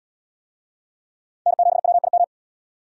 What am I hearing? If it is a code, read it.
Morse code, heard as a keyed tone, about 68 wpm, 700 Hz. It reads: N7WIM